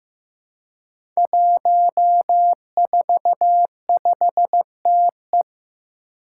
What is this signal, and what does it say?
Morse code, heard as a keyed tone, about 15 wpm, 700 Hz. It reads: 145TE
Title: Five Challenging Letter-Number Combinations 15wpm